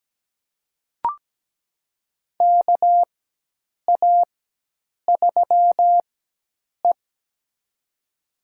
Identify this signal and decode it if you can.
Morse code, heard as a keyed tone, about 17 wpm, spaced as for 8 wpm, 700 Hz. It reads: KA3E
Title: USA Call Signs - 4 Characters - 4x Character Spacing - 17wpm